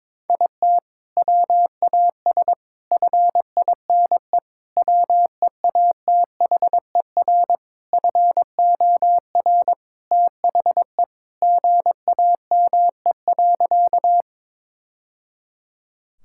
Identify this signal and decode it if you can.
Morse code, heard as a keyed tone, about 21 wpm, 700 Hz. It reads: IT WAS FINE WEATHER FOR THE GAME.
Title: Sentences from Top 1000 Words - Review Part 5 - 22wpm